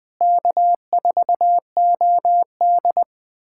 Morse code, heard as a keyed tone, about 21 wpm, 700 Hz. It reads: K4OD
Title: USA Callsign DE Callsign - 4 Characters - 20wpm